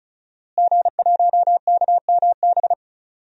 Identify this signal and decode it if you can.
Morse code, heard as a keyed tone, about 36 wpm, 700 Hz. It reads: G1KMB
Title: DXCC-100 Call Signs - 5 Characters - Repeated 2x - 35wpm